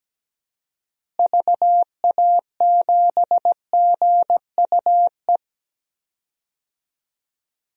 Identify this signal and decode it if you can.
Morse code, heard as a keyed tone, about 17 wpm, 700 Hz. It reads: VA7GUE
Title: DXCC-100 Call Signs - 6 Characters - Repeated 3x - 17wpm